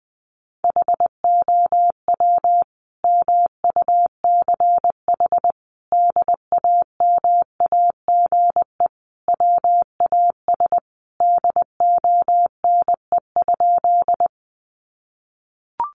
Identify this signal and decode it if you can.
Morse code, heard as a keyed tone, about 20 wpm, 700 Hz. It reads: HOW MUCH DAMAGE WAS DONE?
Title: Sentences from Top 1100 Words 20wpm